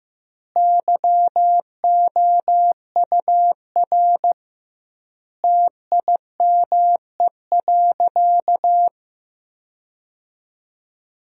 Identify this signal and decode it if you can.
Morse code, heard as a keyed tone, about 16 wpm, 700 Hz. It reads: YOUR TIME.
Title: 2x Word-Spacing - Sentences from Top 100 Words - 15wpm